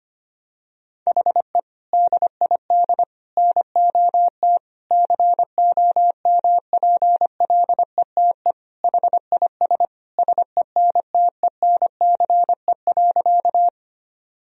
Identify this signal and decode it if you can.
Morse code, heard as a keyed tone, about 25 wpm, 700 Hz. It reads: HE DID NOT COMPLETE HIS SENTENCE.
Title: Sentences from Top 1000 Words - Review Part 1 - 25wpm